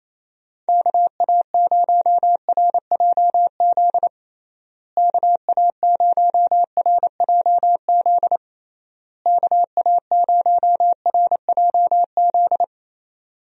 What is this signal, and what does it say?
Morse code, heard as a keyed tone, about 28 wpm, 700 Hz. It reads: KA0RJZ KA0RJZ KA0RJZ
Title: USA Call Signs - 6 Characters - Repeated 3x - 28wpm